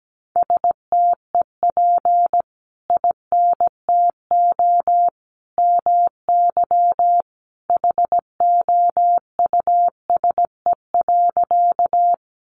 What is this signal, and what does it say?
Morse code, heard as a keyed tone, about 17 wpm, 700 Hz. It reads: STEP INTO MY HOUSE.
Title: Sentences from Top 600 Words 17wpm